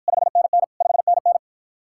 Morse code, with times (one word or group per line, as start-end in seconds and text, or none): 0.1-0.6 5NN
0.8-1.4 5NN